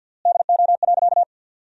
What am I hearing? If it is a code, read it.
Morse code, heard as a keyed tone, about 50 wpm, 700 Hz. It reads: DO.
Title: 3x Word-Spacing - Sentences from Top 100 Words - 50wpm